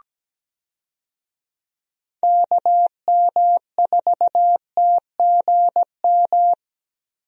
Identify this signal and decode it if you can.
Morse code, heard as a keyed tone, about 17 wpm, 700 Hz. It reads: KM4TGM
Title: USA Call Signs - 6 Characters - Repeated 2x - 17wpm